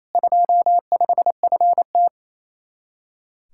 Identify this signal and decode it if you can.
Morse code, heard as a keyed tone, about 35 wpm, 700 Hz. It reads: 25FT